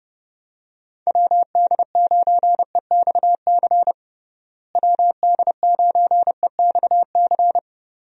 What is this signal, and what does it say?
Morse code, heard as a keyed tone, about 31 wpm, 700 Hz. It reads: WD9EXC WD9EXC
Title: USA Call Signs - 6 Characters - Repeated 3x - 30wpm